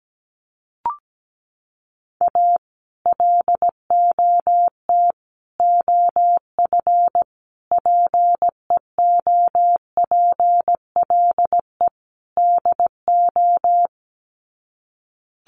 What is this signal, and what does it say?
Morse code, heard as a keyed tone, about 17 wpm, 700 Hz. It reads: A LOT OF PEOPLE DO
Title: Sets of 5 Words 17wpm